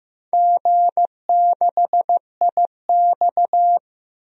Morse code, heard as a keyed tone, about 15 wpm, 700 Hz. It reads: G6IX
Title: DXCC-100 Call Signs - 4 Characters - 15wpm